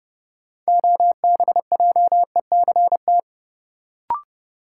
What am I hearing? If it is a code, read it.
Morse code, heard as a keyed tone, about 31 wpm, 700 Hz. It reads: OBJECT
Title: Sentences from Top 500 Words - New Word Drill - 30wpm